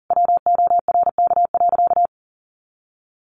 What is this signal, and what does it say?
Morse code, heard as a keyed tone, about 40 wpm, 700 Hz. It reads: WORK.
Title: Sentences from Top 600 Words 40wpm